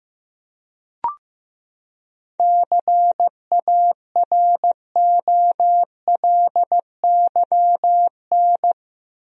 Morse code, heard as a keyed tone, about 16 wpm, 700 Hz. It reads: CAROLYN
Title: USA Common Names - 15wpm